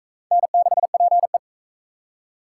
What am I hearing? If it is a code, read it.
Morse code, heard as a keyed tone, about 40 wpm, 700 Hz. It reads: N6PE